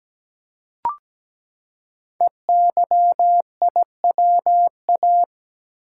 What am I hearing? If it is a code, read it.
Morse code, heard as a keyed tone, about 17 wpm, 700 Hz. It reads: EYIWA
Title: Code Groups - 5 Letters - 17wpm